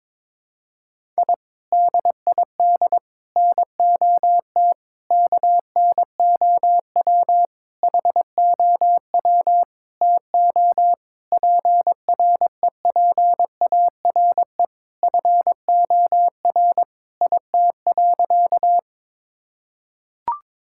Morse code, heard as a keyed tone, about 21 wpm, 700 Hz. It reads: I DID NOT KNOW HOW TO PREPARE FOR IT.
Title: Sentences from Top 1000 Words - Review Part 6 - 22wpm